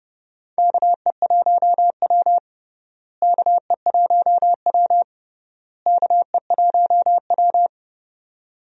Morse code, heard as a keyed tone, about 32 wpm, 700 Hz. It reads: KE1W KE1W KE1W
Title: USA Call Signs - 4 Characters - Repeated 3x - 30wpm